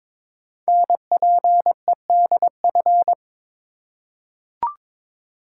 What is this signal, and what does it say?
Morse code, heard as a keyed tone, about 22 wpm, 700 Hz. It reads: NPEDF
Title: Code Groups - 5 Letter:Numbers - 22wpm